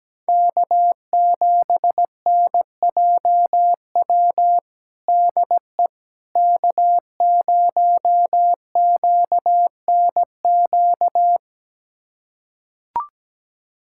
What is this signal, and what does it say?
Morse code, heard as a keyed tone, about 17 wpm, 700 Hz. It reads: K7NJW DE K0QNQ
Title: USA Callsign DE Callsign - 5 Characters - 17wpm